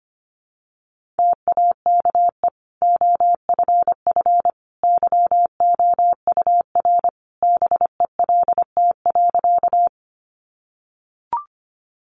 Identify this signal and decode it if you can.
Morse code, heard as a keyed tone, about 25 wpm, 700 Hz. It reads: TAKE OFF YOUR BELT.
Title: Sentences from Top 1400 Words 25wpm